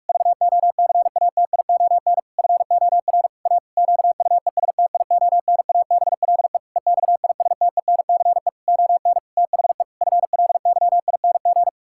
Morse code, heard as a keyed tone, about 46 wpm, 700 Hz. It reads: VOCATION FOR A QUESTIONABLE EXISTENCE ON THE FLYING
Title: Pangram Challenge v2 - 45wpm